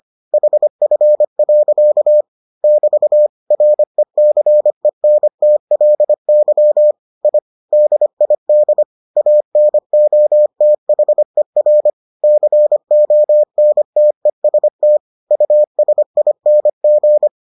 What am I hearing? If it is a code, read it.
Morse code, heard as a keyed tone, about 25 wpm, 600 Hz. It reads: HF. <BT> RECENTLY I DID ANOTHER CONTEST USING